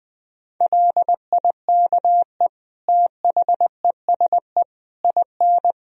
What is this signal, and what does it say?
Morse code, heard as a keyed tone, about 19 wpm, 700 Hz. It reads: LIKE THESE IN
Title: Sentences from Top 500 Words and Abbreviations 20wpm